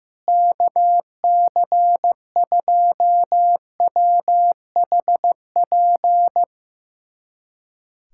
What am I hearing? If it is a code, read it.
Morse code, heard as a keyed tone, about 15 wpm, 700 Hz. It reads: KC2WHP
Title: USA Callsign DE Callsign - 6 Characters - 15wpm